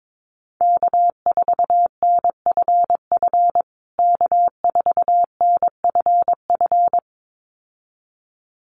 Morse code, heard as a keyed tone, about 22 wpm, 700 Hz. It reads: K4NFF K4NFF
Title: USA Call Signs Repeated 2x 22wpm